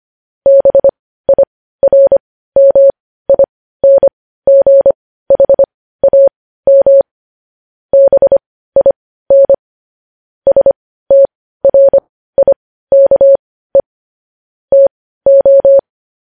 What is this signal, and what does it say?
Morse code, heard as a keyed tone, about 25 wpm, spaced as for 15 wpm, 550 Hz. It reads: BIRMINGHAM BIN STRIKE TO
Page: news headlines in morse code at 15 wpm